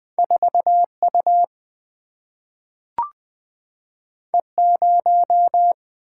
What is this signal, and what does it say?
Morse code, heard as a keyed tone, about 20 wpm, 700 Hz. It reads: 4U E0
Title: Single-Letter Single-Number 20wpm